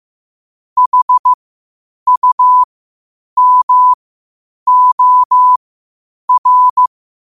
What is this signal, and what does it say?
Morse code, heard as a keyed tone, about 15 wpm, spaced as for 8 wpm, 1000 Hz. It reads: HUMOR
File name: qotdfiles / 8wpm